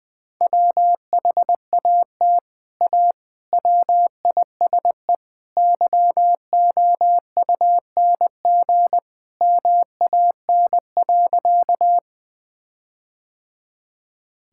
Morse code, heard as a keyed tone, about 20 wpm, 700 Hz. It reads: WHAT A WISE YOUNG MAN.
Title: Sentences from Top 1600 Words 20wpm